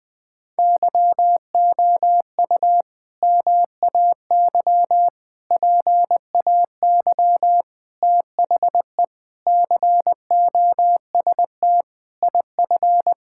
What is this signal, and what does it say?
Morse code, heard as a keyed tone, about 21 wpm, 700 Hz. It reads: YOU MAY PAY THE COST IF
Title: Sentences with 8-Word Phrases - 20wpm